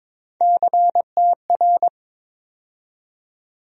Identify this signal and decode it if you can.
Morse code, heard as a keyed tone, about 21 wpm, 700 Hz. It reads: CTR
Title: USA Call Sign Postfixes - 3 Letters 22wpm